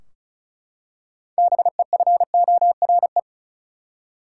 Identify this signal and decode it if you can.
Morse code, heard as a keyed tone, about 34 wpm, 700 Hz. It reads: BEFORE
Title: Six Letter Words - 35wpm